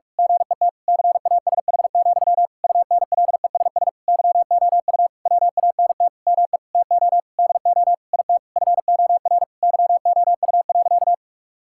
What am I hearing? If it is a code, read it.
Morse code, heard as a keyed tone, about 43 wpm, 700 Hz. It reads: GET CASH, UNLESS YOU WANT ME TO DO IT FOR YOU.
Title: Sentences from Top 1000 Words - Review Part 6 - 45wpm